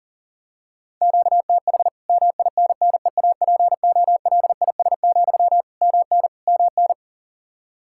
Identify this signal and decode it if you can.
Morse code, heard as a keyed tone, about 40 wpm, 700 Hz. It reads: QTH MINNEAPOLIS, MN MN
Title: Practice QSO - Exchange 1 - Beginning - 40wpm